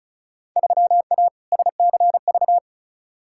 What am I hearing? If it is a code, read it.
Morse code, heard as a keyed tone, about 36 wpm, 700 Hz. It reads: 3A SCV
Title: Field Day Exchanges - 35wpm